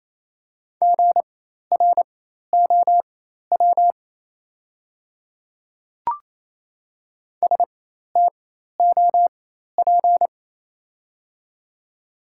Morse code, heard as a keyed tone, about 28 wpm, spaced as for 13 wpm, 700 Hz. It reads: GROW STOP
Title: 4x Character Spacing - Top 300 Words - 28wpm